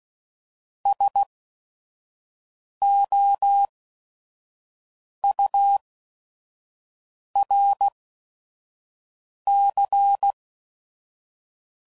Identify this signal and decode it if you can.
Morse code, heard as a keyed tone, about 16 wpm, spaced as for 5 wpm, 800 Hz. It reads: SOURC
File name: CW Code Groups @5WPM / Technical Subjects / XC